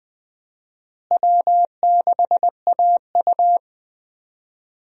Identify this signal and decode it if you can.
Morse code, heard as a keyed tone, about 20 wpm, 700 Hz. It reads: W6AU
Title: USA Call Signs - 4 Characters - Repeated 2x - 20wpm